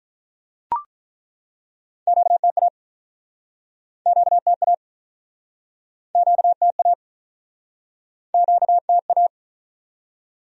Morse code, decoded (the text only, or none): QTA QTA QTA QTA